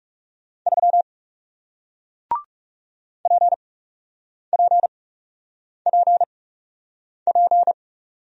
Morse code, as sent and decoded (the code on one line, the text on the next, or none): ...-- / .--. .--. .--. .--.
3 PPPP